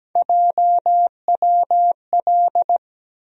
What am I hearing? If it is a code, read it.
Morse code, heard as a keyed tone, about 17 wpm, 700 Hz. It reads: JWL
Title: USA Call Sign Postfixes - 3 Letters 17wpm